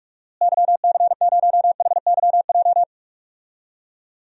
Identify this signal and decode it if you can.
Morse code, heard as a keyed tone, about 46 wpm, 700 Hz. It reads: YC0HYJ
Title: DXCC-100 Call Signs - 6 Characters - Repeated 2x - 45wpm